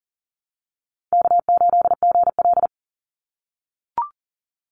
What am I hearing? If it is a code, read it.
Morse code, heard as a keyed tone, about 40 wpm, 700 Hz. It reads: K8GL
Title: Call Signs - 40wpm